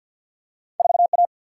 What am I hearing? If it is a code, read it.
Morse code, heard as a keyed tone, about 53 wpm, 700 Hz. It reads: 4 A
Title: Single-Letter Single-Number 50wpm